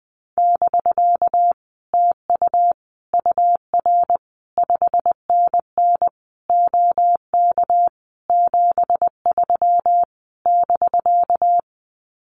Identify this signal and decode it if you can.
Morse code, heard as a keyed tone, about 19 wpm, 700 Hz. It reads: <BK> TU UR 5NN OK 73 <BK>
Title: Common QSO Phrases Encore 20wpm